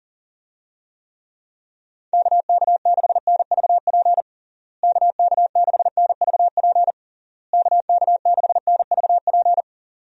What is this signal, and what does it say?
Morse code, heard as a keyed tone, about 40 wpm, 700 Hz. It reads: KK6NVP KK6NVP KK6NVP
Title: USA Call Signs - 6 Characters - Repeated 3x - 40wpm